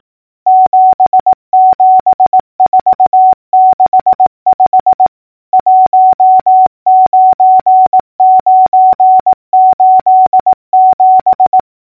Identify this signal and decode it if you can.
Morse code, heard as a keyed tone, about 18 wpm, 750 Hz. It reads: 77465 19987
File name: numbers / V2-morse-clicks